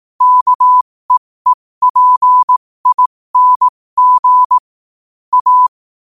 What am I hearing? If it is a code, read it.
Morse code, heard as a keyed tone, about 18 wpm, spaced as for 15 wpm, 1000 Hz. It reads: KEEPING A
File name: qotdfiles / 15wpm